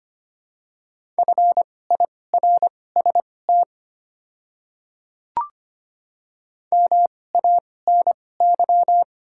Morse code, heard as a keyed tone, about 25 wpm, spaced as for 18 wpm, 690 Hz. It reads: FIRST MANY
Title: 2x Character Spacing - Top 100 Words - 25wpm